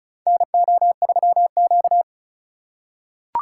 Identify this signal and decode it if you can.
Morse code, heard as a keyed tone, about 34 wpm, 700 Hz. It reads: NO3Q
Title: USA Call Signs - 4 Characters - 2x Character Spacing - 35wpm